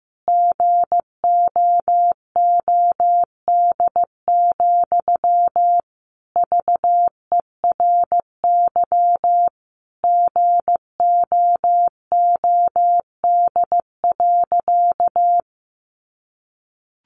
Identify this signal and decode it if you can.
Morse code, heard as a keyed tone, about 15 wpm, 700 Hz. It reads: GOOD, VERY GOOD.
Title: Sentences from Top 200 Words 15wpm